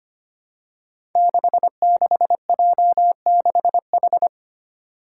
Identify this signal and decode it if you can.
Morse code, heard as a keyed tone, about 25 wpm, 700 Hz. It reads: 66J6H
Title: Five Challenging Letter-Number Combinations 25wpm